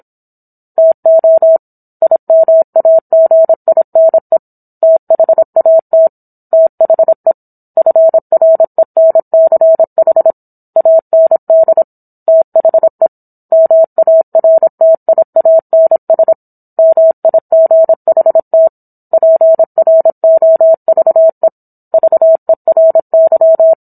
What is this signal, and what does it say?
Morse code, heard as a keyed tone, about 27 wpm, 650 Hz. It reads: TO IMAGINE THAT THE FRENCH AND THE MARTIANS MIGHT PROVE VERY